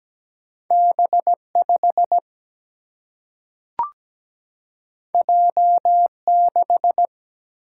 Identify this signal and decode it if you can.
Morse code, heard as a keyed tone, about 17 wpm, 700 Hz. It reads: B5 J6
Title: Single-Letter Single-Number 17wpm